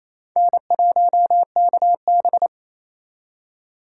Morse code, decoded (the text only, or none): N1KB